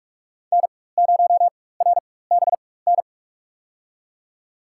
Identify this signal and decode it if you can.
Morse code, heard as a keyed tone, about 48 wpm, spaced as for 21 wpm, 700 Hz. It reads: N0RBN
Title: USA Call Signs - 4x Character Spacing - 45wpm